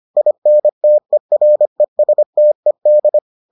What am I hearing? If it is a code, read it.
Morse code, heard as a keyed tone, about 25 wpm, 600 Hz. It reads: INTERESTED